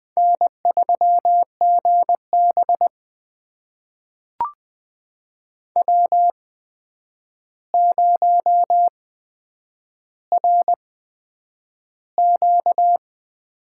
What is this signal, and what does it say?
Morse code, heard as a keyed tone, about 20 wpm, spaced as for 9 wpm, 700 Hz. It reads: N3GB W0RQ
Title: USA Call Signs - 4 Characters - 8x Character Spacing - 20wpm